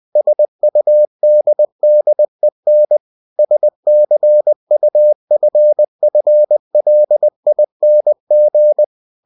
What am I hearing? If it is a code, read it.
Morse code, heard as a keyed tone, about 20 wpm, 600 Hz. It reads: SUDDEN SCUFFLING
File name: cw-books / winnie-the-pooh / winnie-20wpm-page-0013